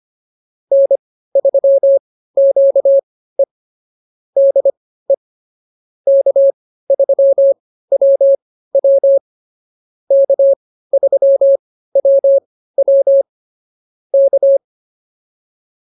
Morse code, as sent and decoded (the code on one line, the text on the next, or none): -. ...-- --.- . / -.. / -.- ...-- .-- .-- / -.- ...-- .-- .-- / -.-
N3QE DE K3WW K3WW K